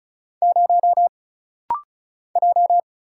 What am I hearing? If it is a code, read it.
Morse code, heard as a keyed tone, about 36 wpm, 700 Hz. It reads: 0 J
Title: Single Letter-Number - Rapid-Fire 35wpm